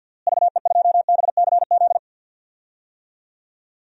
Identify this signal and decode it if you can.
Morse code, heard as a keyed tone, about 52 wpm, 700 Hz. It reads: VE2BCZ
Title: DXCC-100 Call Signs - 6 Characters - Repeated 2x - 50wpm